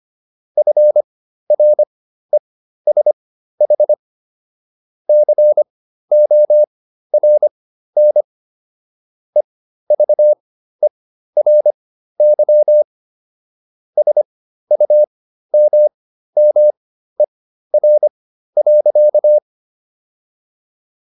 Morse code, heard as a keyed tone, about 25 wpm, spaced as for 13 wpm, 600 Hz. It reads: FRESH CORN EVERY SUMMER.